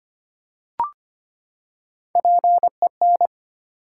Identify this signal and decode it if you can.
Morse code, heard as a keyed tone, about 25 wpm, 700 Hz. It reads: PEN
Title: Three Letter Words - 25wpm